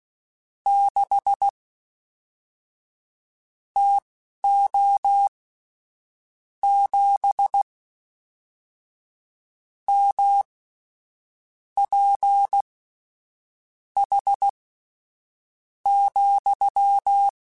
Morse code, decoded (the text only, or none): 6 TO 7 MPH,